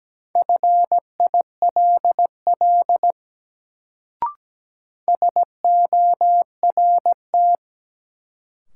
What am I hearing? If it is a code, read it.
Morse code, heard as a keyed tone, about 17 wpm, 700 Hz. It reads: FILL SORT
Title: Sentences from Top 1000 Words - Review Part 3 - New Word Drill - 17wpm